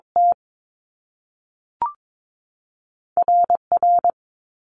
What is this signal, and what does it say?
Morse code, heard as a keyed tone, about 22 wpm, spaced as for 3 wpm, 700 Hz. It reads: TR R
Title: Top Two Letter Combinations in Words 22wpm